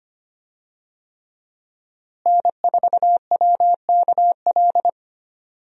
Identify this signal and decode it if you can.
Morse code, heard as a keyed tone, about 25 wpm, 700 Hz. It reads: N4WKL
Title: USA Call Signs - 5 Characters - Repeated 3x - 25wpm